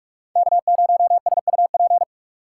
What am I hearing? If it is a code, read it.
Morse code, heard as a keyed tone, about 57 wpm, 700 Hz. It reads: K0SUP